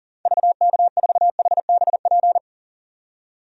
Morse code, heard as a keyed tone, about 40 wpm, 700 Hz. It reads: VK4HBP
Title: DXCC-100 Call Signs - 6 Characters - 40wpm